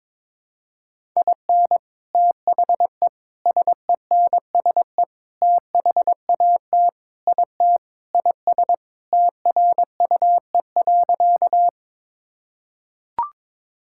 Morse code, heard as a keyed tone, about 22 wpm, 700 Hz. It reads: IN THE SENSE THAT IT IS TRUE.
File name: Sentences with 4-Word Phrases - 22wpm